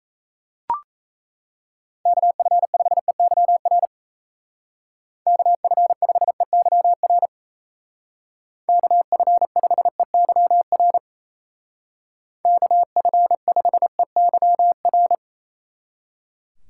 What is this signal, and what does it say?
Morse code, heard as a keyed tone, about 34 wpm, 700 Hz. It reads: KF5EYR KF5EYR KF5EYR KF5EYR
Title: Speed Racing - USA 2x3 Call Signs 28wpm